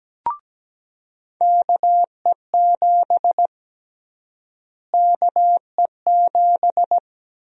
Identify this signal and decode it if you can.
Morse code, heard as a keyed tone, about 17 wpm, 700 Hz. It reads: KE7 KE7
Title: USA Call Sign Prefixes - Repeated 2x - 17wpm